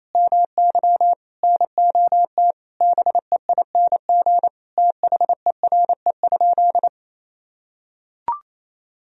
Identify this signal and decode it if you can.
Morse code, heard as a keyed tone, about 28 wpm, 700 Hz. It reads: MY NOT BEING THERE?
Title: Sentences from Top 1000 Words - Review Part 3 - 28wpm